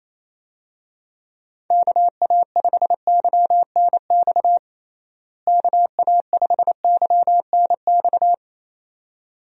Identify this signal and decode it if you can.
Morse code, heard as a keyed tone, about 28 wpm, 700 Hz. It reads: KA5YNX KA5YNX
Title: USA Call Signs - 6 Characters - Repeated 2x - 28wpm